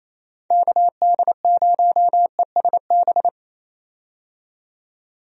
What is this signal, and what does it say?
Morse code, heard as a keyed tone, about 27 wpm, 700 Hz. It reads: KD0ESB
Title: USA Call Signs - 6 Characters - 28wpm